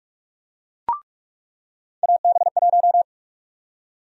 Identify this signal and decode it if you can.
Morse code, heard as a keyed tone, about 46 wpm, 700 Hz. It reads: AB1
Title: USA Call Sign Prefixes - 45wpm